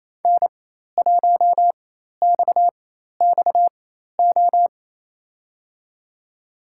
Morse code, heard as a keyed tone, about 28 wpm, spaced as for 13 wpm, 700 Hz. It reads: N1XXO
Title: USA Call Signs - 5 Characters - 4x Character Spacing - 28wpm